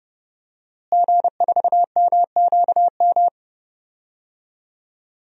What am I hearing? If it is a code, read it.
Morse code, heard as a keyed tone, about 30 wpm, 700 Hz. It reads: G4MQM
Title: DXCC-100 Call Signs - 5 Characters - Repeated 3x - 30wpm